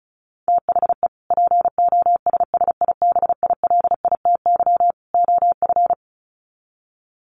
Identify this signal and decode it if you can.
Morse code, heard as a keyed tone, about 34 wpm, 700 Hz. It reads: THE POSSIBILITY OF
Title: Sets of 3 Words 35wpm